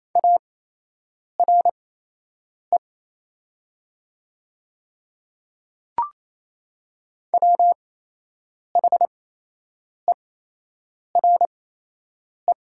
Morse code, heard as a keyed tone, about 28 wpm, spaced as for 8 wpm, 700 Hz. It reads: ARE WHERE
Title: 8x Character Spacing - Top 200 Words - 28wpm